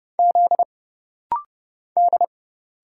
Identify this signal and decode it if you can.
Morse code, heard as a keyed tone, about 30 wpm, 700 Hz. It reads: Z D
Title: Single Letters - Rapid-Fire 30wpm